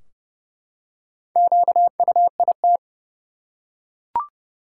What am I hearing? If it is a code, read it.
Morse code, heard as a keyed tone, about 30 wpm, 700 Hz. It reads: QUIT
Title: Four Letter Words - 30wpm